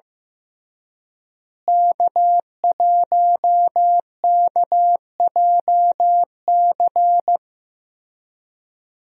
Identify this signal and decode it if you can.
Morse code, heard as a keyed tone, about 15 wpm, 700 Hz. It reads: K1KJC
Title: USA Call Signs - 5 Characters - Repeated 3x - 15wpm